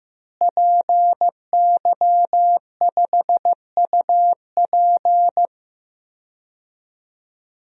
Morse code, heard as a keyed tone, about 15 wpm, 700 Hz. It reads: PY5UP
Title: DXCC-100 Call Signs - 5 Characters - Repeated 3x - 15wpm